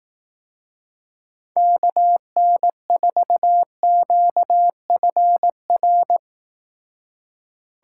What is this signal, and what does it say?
Morse code, heard as a keyed tone, about 19 wpm, 700 Hz. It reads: KN4QFR
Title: Speed Racing - USA Call Signs 15wpm